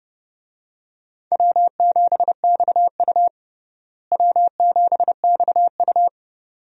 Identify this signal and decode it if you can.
Morse code, heard as a keyed tone, about 32 wpm, 700 Hz. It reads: W7XU W7XU
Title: USA Call Signs - 4 Characters - Repeated 2x - 30wpm